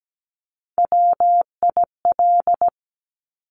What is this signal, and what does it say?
Morse code, heard as a keyed tone, about 17 wpm, 700 Hz. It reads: WIL
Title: Top Three Letter Combinations in Words 17wpm